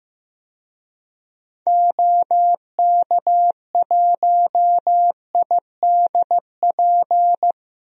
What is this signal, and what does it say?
Morse code, heard as a keyed tone, about 15 wpm, 700 Hz. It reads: OK1IDP
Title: DXCC-100 Call Signs - 6 Characters - 15wpm